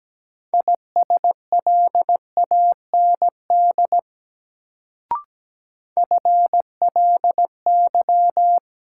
Morse code, heard as a keyed tone, about 17 wpm, 700 Hz. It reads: ISLAND FLY
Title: Sentences from Top 500 Words - New Word Drill - 17wpm